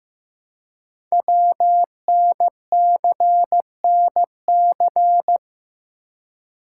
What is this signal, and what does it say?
Morse code, heard as a keyed tone, about 15 wpm, 700 Hz. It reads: WNCNC
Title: Code Groups - 5 Letters - 15wpm